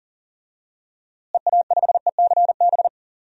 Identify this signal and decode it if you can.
Morse code, heard as a keyed tone, about 40 wpm, 700 Hz. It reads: EA5ECB
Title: DXCC-100 Call Signs - 6 Characters - Repeated 3x - 40wpm